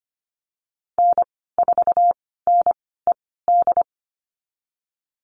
Morse code, heard as a keyed tone, about 25 wpm, spaced as for 16 wpm, 700 Hz. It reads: N4NED
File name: Lesson 15g - Number 5 - Call Signs 25wpm